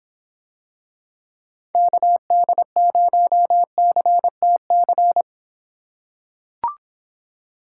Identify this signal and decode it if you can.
Morse code, heard as a keyed tone, about 26 wpm, 700 Hz. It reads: KD0CTC